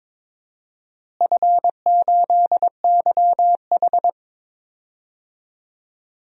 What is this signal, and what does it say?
Morse code, heard as a keyed tone, about 22 wpm, 700 Hz. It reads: F8YH
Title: DXCC-100 Call Signs - 4 Characters - Repeated 2x - 22wpm